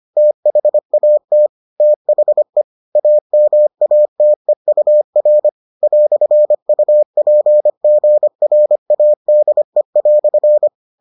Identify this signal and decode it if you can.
Morse code, heard as a keyed tone, about 25 wpm, 600 Hz. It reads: THAT THE AMATEUR "UPGRADE"